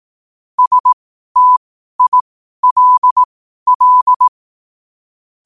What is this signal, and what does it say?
Morse code, heard as a keyed tone, about 18 wpm, spaced as for 12 wpm, 1000 Hz. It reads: STILL
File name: qotdfiles / 12wpm